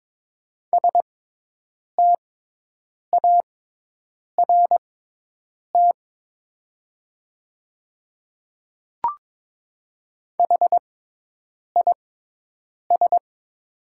Morse code, heard as a keyed tone, about 23 wpm, spaced as for 8 wpm, 700 Hz. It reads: START HIS
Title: 6x Character Spacing - Top 500 Words - 22wpm